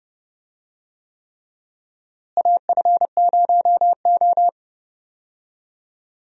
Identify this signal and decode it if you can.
Morse code, heard as a keyed tone, about 31 wpm, 700 Hz. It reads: AF0O
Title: USA Call Signs - 4 Characters - Repeated 3x - 30wpm